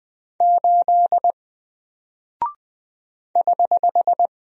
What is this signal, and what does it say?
Morse code, heard as a keyed tone, about 20 wpm, 700 Hz. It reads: 8 <HH>
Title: Single Character (Letter:Number:Punctuation:Prosign) Alt 20wpm